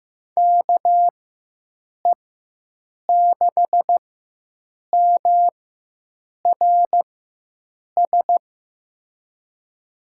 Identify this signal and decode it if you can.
Morse code, heard as a keyed tone, about 15 wpm, spaced as for 7 wpm, 700 Hz. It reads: KE6MRS
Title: USA Call Signs - 6 Characters - 4x Character Spacing - 15wpm